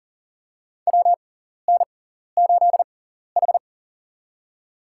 Morse code, heard as a keyed tone, about 40 wpm, spaced as for 14 wpm, 700 Hz. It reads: WN8H
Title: USA Call Signs - 4 Characters - 6x Character Spacing - 40wpm